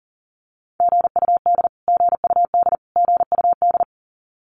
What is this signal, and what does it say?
Morse code, heard as a keyed tone, about 40 wpm, 700 Hz. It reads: GUD GUD GUD
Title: Abbreviations Repeated 3x - 40wpm